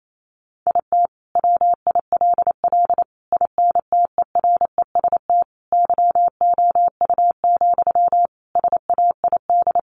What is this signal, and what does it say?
Morse code, heard as a keyed tone, about 28 wpm, 700 Hz. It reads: IT WILL INTEREST YOU, SAID
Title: Sentences from Top 400 Words 28wpm